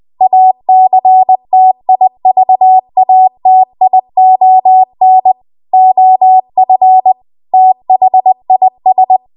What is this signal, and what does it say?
Morse code, heard as a keyed tone, about 20 wpm, 750 Hz. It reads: ACTIVATION OF THIS
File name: Archive / 20 WPM / 20WPM